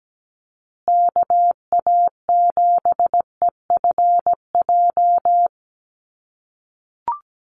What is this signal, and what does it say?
Morse code, heard as a keyed tone, about 17 wpm, 700 Hz. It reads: KA7EFJ
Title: USA Call Signs Repeated 2x 17wpm